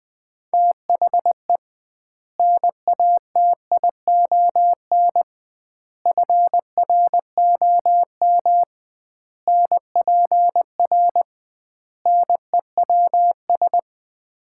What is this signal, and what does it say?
Morse code, heard as a keyed tone, about 20 wpm, 700 Hz. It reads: THE NATION FROM NPR NEWS
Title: 2x Word-Spacing - Sets of 5 Words - 20wpm